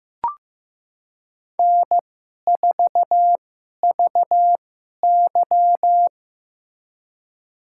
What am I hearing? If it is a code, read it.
Morse code, heard as a keyed tone, about 15 wpm, spaced as for 11 wpm, 700 Hz. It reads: N4VY
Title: USA Call Signs - 4 Characters - 2x Character Spacing - 15wpm